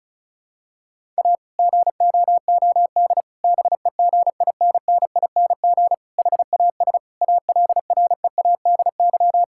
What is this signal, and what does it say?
Morse code, heard as a keyed tone, about 36 wpm, 690 Hz. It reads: A GOOD BEGINNING HAS ALREADY
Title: Sentences from Top 1000 Words - Review Part 5 - 35wpm